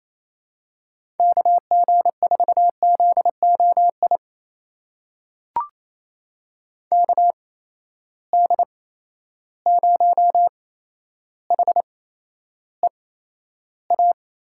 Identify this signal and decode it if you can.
Morse code, heard as a keyed tone, about 28 wpm, spaced as for 12 wpm, 700 Hz. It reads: KG4ZOI KD0HEA
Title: USA Call Signs - 6 Characters - 8x Character Spacing - 28wpm